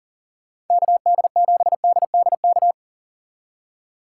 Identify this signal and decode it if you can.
Morse code, heard as a keyed tone, about 51 wpm, 700 Hz. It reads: KD7DDK